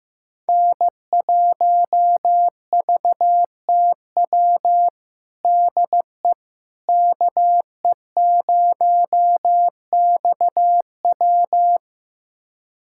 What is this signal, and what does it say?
Morse code, heard as a keyed tone, about 16 wpm, 700 Hz. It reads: N1VTW DE KE0XW
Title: USA Callsign DE Callsign - 5 Characters - 15wpm